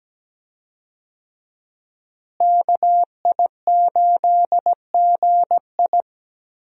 Morse code, heard as a keyed tone, about 17 wpm, 700 Hz. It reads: KI8GI
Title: USA Call Signs - 5 Characters - Repeated 2x - 17wpm